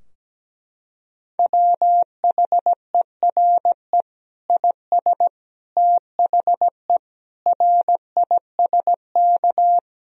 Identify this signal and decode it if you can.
Morse code, heard as a keyed tone, about 17 wpm, 700 Hz. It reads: WHERE IS THE RISK
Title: Sentences from Top 1000 Words - Review Part 4 - 17wpm